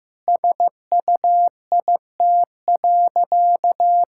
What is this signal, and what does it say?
Morse code, heard as a keyed tone, about 15 wpm, 700 Hz. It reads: SUIT.
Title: Sentences from Top 1600 Words 15wpm